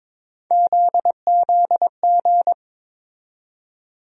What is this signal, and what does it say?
Morse code, heard as a keyed tone, about 22 wpm, 700 Hz. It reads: ZZG
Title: USA Call Sign Postfixes - 3 Letters 22wpm